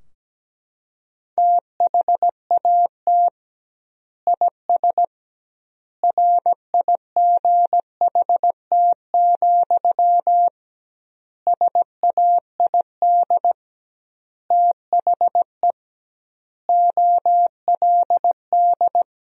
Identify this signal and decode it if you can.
Morse code, heard as a keyed tone, about 17 wpm, 700 Hz. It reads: THAT IS RIGHT, SAID THE OLD
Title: 2x Word-Spacing - Sentences from Top 200 Words - 17wpm